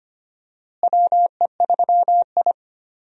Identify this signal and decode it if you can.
Morse code, heard as a keyed tone, about 25 wpm, 700 Hz. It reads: WE3I
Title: USA Call Signs - 4 Characters - Repeated 3x - 25wpm